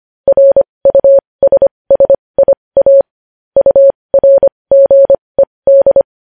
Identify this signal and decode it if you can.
Morse code, heard as a keyed tone, about 25 wpm, spaced as for 20 wpm, 550 Hz. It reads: RUSSIA URGED
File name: podcast / News headlines-1x-25x20